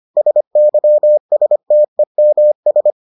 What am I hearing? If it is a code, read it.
Morse code, heard as a keyed tone, about 25 wpm, 600 Hz. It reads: SYSTEMS